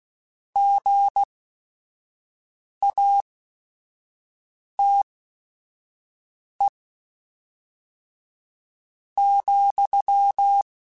Morse code, heard as a keyed tone, about 16 wpm, spaced as for 4 wpm, 800 Hz. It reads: GATE,
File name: Technical Subjects / Technical Subjects-mp3s / AND Gate @5